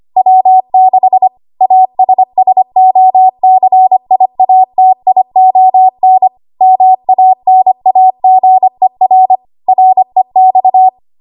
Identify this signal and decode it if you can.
Morse code, heard as a keyed tone, about 25 wpm, 750 Hz. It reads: W6 ASSOCIATION MANAGER REX
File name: Morse / 25WPM